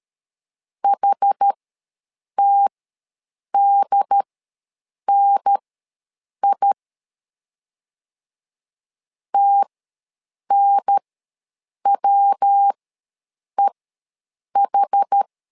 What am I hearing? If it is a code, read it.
Morse code, heard as a keyed tone, about 13 wpm, spaced as for 7 wpm, 800 Hz. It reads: HTDNI TNWEH
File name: LEZIONI / 3 MWRO / 3 MW ws 12wpm 1816